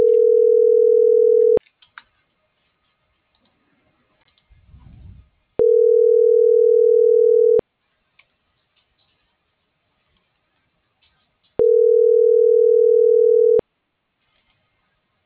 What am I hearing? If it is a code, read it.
Ringback tone (North America), 440 + 480 Hz.